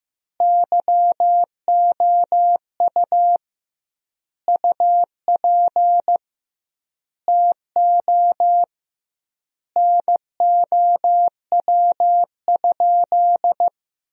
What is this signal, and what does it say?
Morse code, heard as a keyed tone, about 15 wpm, 700 Hz. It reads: YOU UP TO NOW?
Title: 2x Word-Spacing - Sentences from Top 100 Words - 15wpm